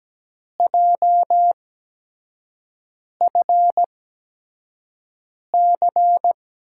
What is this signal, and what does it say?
Morse code, heard as a keyed tone, about 17 wpm, spaced as for 5 wpm, 700 Hz. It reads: JFC
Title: USA Call Signs - 5 Characters - 8x Character Spacing - 17wpm